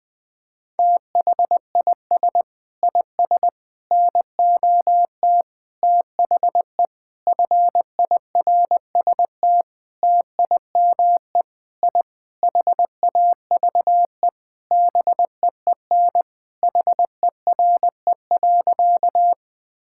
Morse code, heard as a keyed tone, about 20 wpm, 700 Hz. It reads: THIS IS NOT THE FIRST TIME I HAVE BEEN HERE.
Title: Sentences with 6-Word Phrases - 20wpm